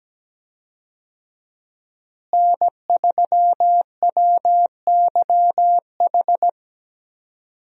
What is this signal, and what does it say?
Morse code, heard as a keyed tone, about 17 wpm, 700 Hz. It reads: N3WYH
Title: USA Call Signs - 5 Characters - Repeated 3x - 17wpm